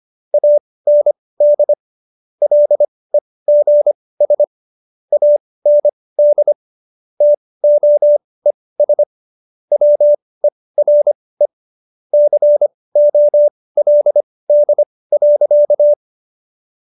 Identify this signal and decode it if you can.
Morse code, heard as a keyed tone, about 25 wpm, spaced as for 18 wpm, 590 Hz. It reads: AND LEGS AND TOES WERE COLD.